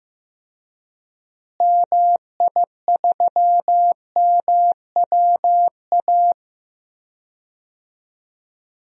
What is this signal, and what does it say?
Morse code, heard as a keyed tone, about 15 wpm, 700 Hz. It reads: MI3MWA
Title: DXCC-100 Call Signs - 6 Characters - Repeated 3x - 15wpm